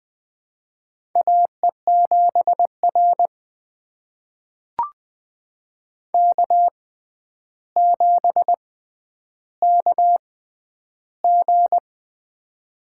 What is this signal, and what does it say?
Morse code, heard as a keyed tone, about 20 wpm, spaced as for 10 wpm, 700 Hz. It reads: AE7R K7KG
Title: USA Call Signs - 4 Characters - 6x Character Spacing - 20wpm